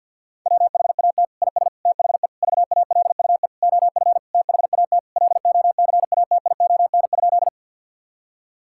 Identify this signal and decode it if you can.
Morse code, heard as a keyed tone, about 50 wpm, 700 Hz. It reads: WHAT IS THE VALUE OF THAT LOCATION?
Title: Sentences from Top 1000 Words - Review Part 6 - 50wpm